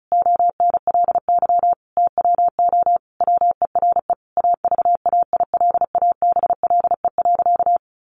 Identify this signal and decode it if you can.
Morse code, heard as a keyed tone, about 34 wpm, 700 Hz. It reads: ONLY TWO WERE AVAILABLE.
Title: Sentences from Top 1100 Words 35wpm